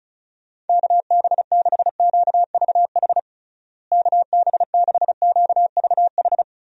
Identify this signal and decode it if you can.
Morse code, heard as a keyed tone, about 36 wpm, 700 Hz. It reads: KB6QVH KB6QVH
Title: USA Call Signs - 6 Characters - Repeated 3x - 35wpm